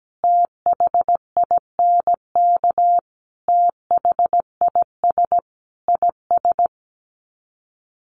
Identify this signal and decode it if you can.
Morse code, heard as a keyed tone, about 17 wpm, 700 Hz. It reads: THINK THIS IS
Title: Sets of 5 Words Encore2 17wpm